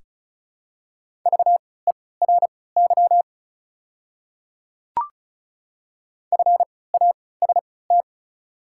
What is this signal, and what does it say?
Morse code, heard as a keyed tone, about 34 wpm, spaced as for 20 wpm, 700 Hz. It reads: VERY FAST
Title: 3x Character Spacing - Top 500 Words - 35wpm